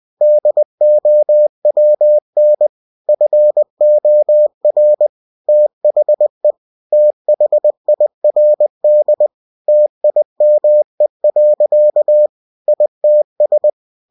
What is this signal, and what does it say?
Morse code, heard as a keyed tone, about 20 wpm, 600 Hz. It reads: DOWN FOR THE THIRD TIME. ITS
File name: cw-books / winnie-the-pooh / winnie-20wpm-page-0054